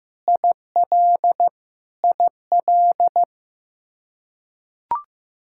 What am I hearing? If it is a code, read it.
Morse code, heard as a keyed tone, about 15 wpm, 700 Hz. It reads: IL IL
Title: Practice QSO - Exchange 1 - Beginning - 15wpm